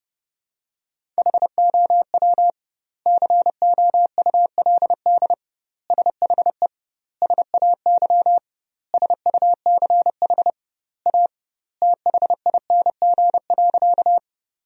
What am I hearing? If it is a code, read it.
Morse code, heard as a keyed tone, about 30 wpm, 700 Hz. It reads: HOW COULD SHE SAY SUCH A THING.
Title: 2x Word-Spacing - Sentences from Top 200 Words - 30wpm